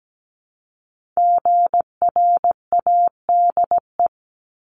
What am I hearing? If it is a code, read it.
Morse code, heard as a keyed tone, about 17 wpm, 700 Hz. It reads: GRADE
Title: Multi-letter Phonemes Repeated 3x 17wpm